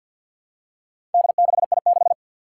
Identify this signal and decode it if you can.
Morse code, heard as a keyed tone, about 51 wpm, 700 Hz. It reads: D6I6
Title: Four Challenging Letter-Number Combinations 50wpm